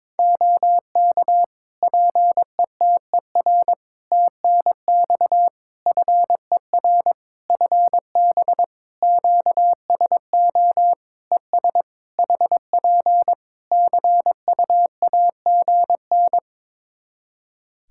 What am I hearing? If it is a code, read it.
Morse code, heard as a keyed tone, about 22 wpm, 700 Hz. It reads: OK PETER TNX FER FB QSO ES HP CUAGN
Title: K1USN SST QSO Practice 22wpm